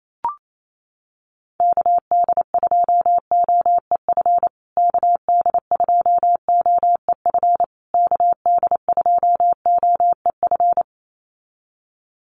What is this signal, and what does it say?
Morse code, heard as a keyed tone, about 28 wpm, 700 Hz. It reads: KD2OEF KD2OEF KD2OEF
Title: USA Call Signs Repeated 3x 28wpm